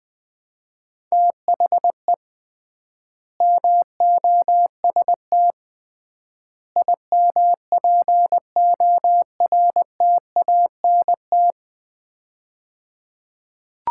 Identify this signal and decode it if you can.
Morse code, heard as a keyed tone, about 20 wpm, 700 Hz. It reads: THE MOST IMPORTANT
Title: 3x Word-Spacing - Sets of 3 Words - 20wpm